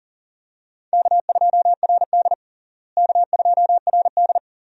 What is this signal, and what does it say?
Morse code, heard as a keyed tone, about 40 wpm, 700 Hz. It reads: K2RD K2RD
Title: USA Call Signs - 4 Characters - Repeated 3x - 40wpm